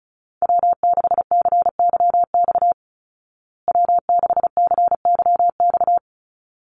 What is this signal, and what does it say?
Morse code, heard as a keyed tone, about 34 wpm, 700 Hz. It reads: W6CYX W6CYX
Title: Call Signs Repeated 3x - 35wpm